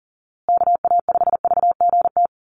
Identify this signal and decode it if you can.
Morse code, heard as a keyed tone, about 40 wpm, 700 Hz. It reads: KA5VGT
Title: USA Call Signs Repeated 3x 40wpm